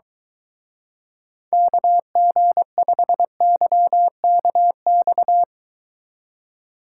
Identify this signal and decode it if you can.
Morse code, heard as a keyed tone, about 23 wpm, 700 Hz. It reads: KG5YKX